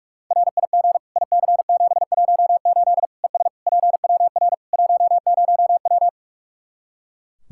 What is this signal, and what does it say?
Morse code, heard as a keyed tone, about 41 wpm, 700 Hz. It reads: RIG IC718 ES PWR 10W
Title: Practice QSO - Exchange 2 - Info - 45wpm